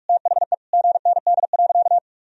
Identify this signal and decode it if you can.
Morse code, heard as a keyed tone, about 46 wpm, 700 Hz. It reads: THE GND.
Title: Sentences from Top 500 Words and Abbreviations 45wpm